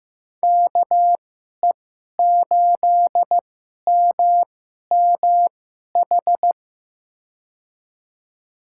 Morse code, heard as a keyed tone, about 15 wpm, spaced as for 11 wpm, 700 Hz. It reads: KE8MMH
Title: USA Call Signs - 6 Characters - 2x Character Spacing - 15wpm